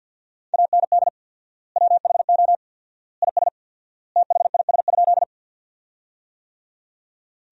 Morse code, heard as a keyed tone, about 50 wpm, 700 Hz. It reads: AND WHO IS THIS?
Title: 4x Word-Spacing - Sentences from Top 100 Words - 50wpm